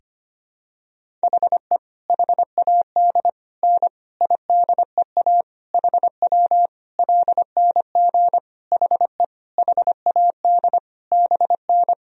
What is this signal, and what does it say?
Morse code, heard as a keyed tone, about 25 wpm, 700 Hz. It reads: HE HAD N IDEA HW LNG HE HAD BN
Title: Sentences from Top 300 Words and Abbreviations 25wpm